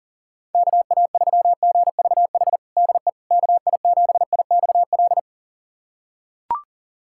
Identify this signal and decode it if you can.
Morse code, heard as a keyed tone, about 41 wpm, 690 Hz. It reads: KA3GVH DE KI7IXL
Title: USA Callsign DE Callsign - 6 Characters - 40wpm